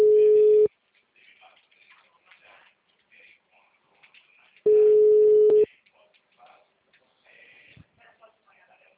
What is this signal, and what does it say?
Ringback tone (Europe, 425 Hz), with speech.
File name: out- -2002- - - .65005